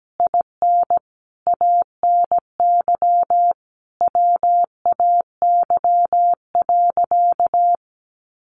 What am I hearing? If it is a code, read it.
Morse code, heard as a keyed tone, about 17 wpm, 700 Hz. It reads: IN ANY WAY.
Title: Sentences from Top 1100 Words 17wpm